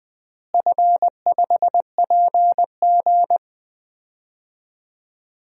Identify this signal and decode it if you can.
Morse code, heard as a keyed tone, about 21 wpm, 700 Hz. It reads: F5PG
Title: DXCC-100 Call Signs - 4 Characters - Repeated 2x - 20wpm